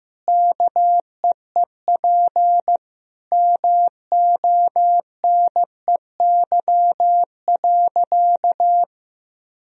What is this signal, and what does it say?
Morse code, heard as a keyed tone, about 16 wpm, 700 Hz. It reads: KEEP MONEY.
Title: Sentences from Top 1000 Words - Review Part 4 - 15wpm